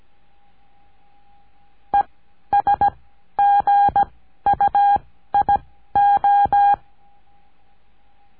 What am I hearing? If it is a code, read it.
Morse code, heard as a keyed tone, about 17 wpm, spaced as for 12 wpm, 810 Hz. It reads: ESGUIO
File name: cursos / CURSO DE TELEGRAFIA / Telegrafia - Trilha 14